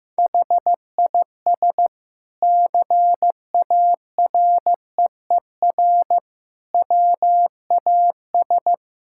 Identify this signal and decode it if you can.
Morse code, heard as a keyed tone, about 16 wpm, 700 Hz. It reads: HIS CAREER WAS
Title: Sentences from Top 1000 Words - Review Part 6 - 15wpm